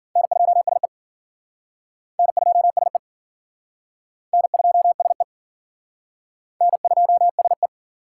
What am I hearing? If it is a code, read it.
Morse code, heard as a keyed tone, about 51 wpm, 700 Hz. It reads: N2SE N2SE N2SE N2SE